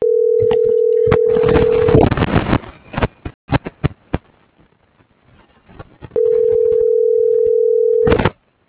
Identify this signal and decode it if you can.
Ringback tone (North America), 440 + 480 Hz.